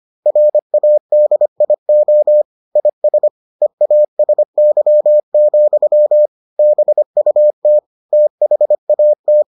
Morse code, heard as a keyed tone, about 25 wpm, 600 Hz. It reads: RADIO IS EASY, BUT THAT